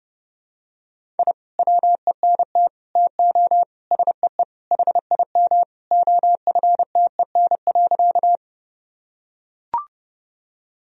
Morse code, heard as a keyed tone, about 30 wpm, 700 Hz. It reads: I WENT TO SEE HIM OFTEN.
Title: Sentences from Top 1000 Words - Review Part 1 - 30wpm